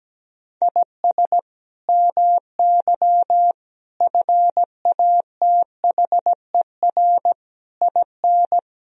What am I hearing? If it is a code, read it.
Morse code, heard as a keyed tone, about 17 wpm, 700 Hz. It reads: IS MY FATHER IN
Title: Sentences from Top 1000 Words - Review Part 2 - 17wpm